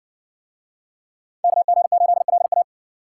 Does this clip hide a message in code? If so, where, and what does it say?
Yes: Morse code from 1.4 to 2.6 s, heard as a keyed tone, about 60 wpm, 700 Hz. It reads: KG8LA